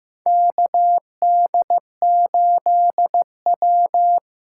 Morse code, heard as a keyed tone, about 15 wpm, 700 Hz. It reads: KD8W
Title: USA Call Signs - 6 Characters - 6x Character Spacing - 15wpm